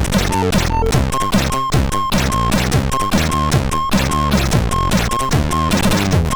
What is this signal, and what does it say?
There is no code.